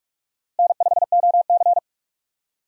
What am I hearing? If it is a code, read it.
Morse code, heard as a keyed tone, about 46 wpm, 700 Hz. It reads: N5OC
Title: USA Call Signs - 4 Characters - Repeated 2x - 45wpm